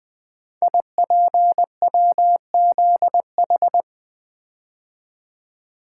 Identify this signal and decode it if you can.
Morse code, heard as a keyed tone, about 20 wpm, 700 Hz. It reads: IPWZH
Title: Code Groups - 5 Letters - 20wpm